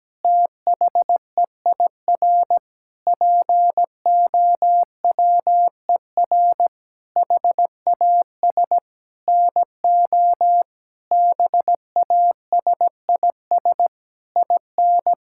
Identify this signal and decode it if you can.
Morse code, heard as a keyed tone, about 17 wpm, 700 Hz. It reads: THEIR POWER HAS NO BASIS IN
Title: Sentences from Top 1000 Words - Review Part 5 - 17wpm